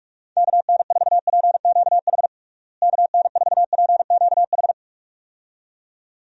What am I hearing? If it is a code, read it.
Morse code, heard as a keyed tone, about 42 wpm, 700 Hz. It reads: KN4PQH KN4PQH
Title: USA Call Signs - 6 Characters - Repeated 2x - 45wpm